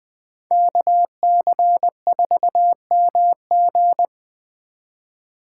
Morse code, heard as a keyed tone, about 22 wpm, 700 Hz. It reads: KC4MG